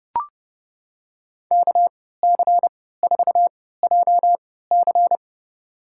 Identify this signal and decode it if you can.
Morse code, heard as a keyed tone, about 30 wpm, spaced as for 17 wpm, 700 Hz. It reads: KC4JC
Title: USA Call Signs - 3x Character Spacing - 30wpm